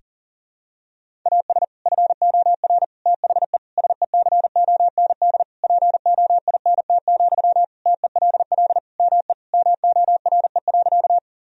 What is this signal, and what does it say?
Morse code, heard as a keyed tone, about 37 wpm, 700 Hz. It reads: AS FOR THE SECOND POINT, TELL ME MORE.
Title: Sentences from Top 1000 Words - Review Part 1 - 40wpm